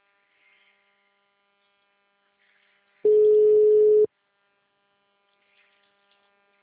Ringback tone (Europe, 425 Hz).